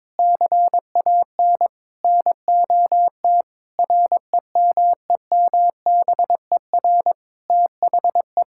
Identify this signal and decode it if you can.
Morse code, heard as a keyed tone, about 21 wpm, 700 Hz. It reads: CAN NOT REMEMBER THE
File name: practice / 22wpm / Sets of 8 Words - 22wpm